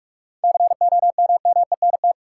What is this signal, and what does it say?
Morse code, heard as a keyed tone, about 46 wpm, 700 Hz. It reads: COMMENT
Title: Seven Letter Words - 45wpm